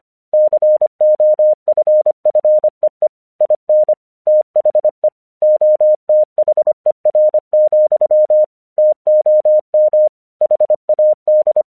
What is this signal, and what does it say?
Morse code, heard as a keyed tone, about 26 wpm, 610 Hz. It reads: COFFEE IN THE OTHER, TOM HAD
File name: 1 English 25wpm / AQC23